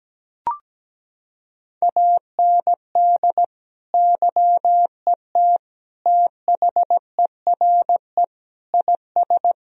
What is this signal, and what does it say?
Morse code, heard as a keyed tone, about 17 wpm, 700 Hz. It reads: AND YET THERE IS
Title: Sentences from Top 1000 Words - Review Part 1 - 17wpm